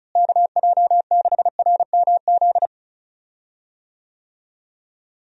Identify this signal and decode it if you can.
Morse code, heard as a keyed tone, about 34 wpm, 700 Hz. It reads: KJ6RMZ
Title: USA Call Signs - 6 Characters - Repeated 3x - 35wpm